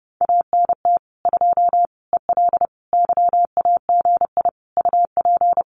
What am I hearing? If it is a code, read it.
Morse code, heard as a keyed tone, about 30 wpm, 700 Hz. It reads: ANT 2 EL YAGI UP
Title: Common QSO Phrases Encore 30wpm